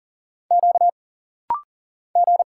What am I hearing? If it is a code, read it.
Morse code, heard as a keyed tone, about 40 wpm, 700 Hz. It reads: Q G
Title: Single Letters - Rapid-Fire 40wpm